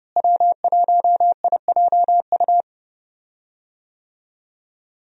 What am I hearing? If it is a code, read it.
Morse code, heard as a keyed tone, about 32 wpm, 700 Hz. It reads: W1IJU
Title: USA Call Signs - 5 Characters - Repeated 3x - 30wpm